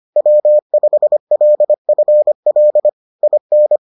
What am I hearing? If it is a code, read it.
Morse code, heard as a keyed tone, about 25 wpm, 600 Hz. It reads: W5LFL IN